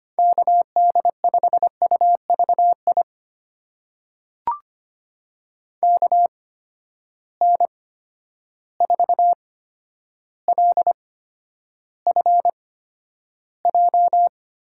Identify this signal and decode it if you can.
Morse code, heard as a keyed tone, about 25 wpm, spaced as for 11 wpm, 700 Hz. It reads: KD5UVI KN4LFJ
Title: USA Call Signs - 6 Characters - 8x Character Spacing - 25wpm